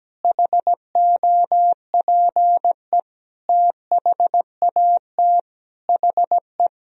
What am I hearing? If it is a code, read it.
Morse code, heard as a keyed tone, about 17 wpm, 700 Hz. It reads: HOPE THAT HE
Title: Sentences from Top 1000 Words - Review Part 1 - 17wpm